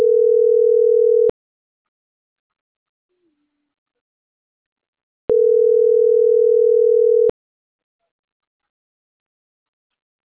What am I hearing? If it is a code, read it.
Ringback tone (North America), 440 + 480 Hz.